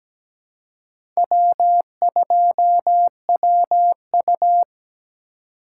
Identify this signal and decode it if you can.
Morse code, heard as a keyed tone, about 17 wpm, 700 Hz. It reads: W2WU
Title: USA Call Signs - 4 Characters - 2x Character Spacing - 17wpm